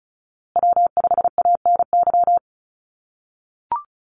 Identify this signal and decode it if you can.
Morse code, heard as a keyed tone, about 34 wpm, 700 Hz. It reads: W5ANY
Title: USA Call Signs Repeated 3x 35wpm